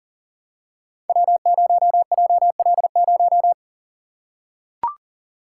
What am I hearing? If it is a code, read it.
Morse code, heard as a keyed tone, about 39 wpm, 700 Hz. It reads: W0JL0
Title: Code Groups - 5 Letter:Numbers - 40wpm